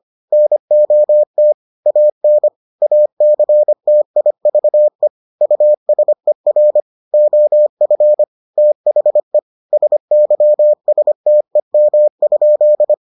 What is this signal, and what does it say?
Morse code, heard as a keyed tone, about 25 wpm, 600 Hz. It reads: NOT AN ACTIVE USER OF THE SYSTEM?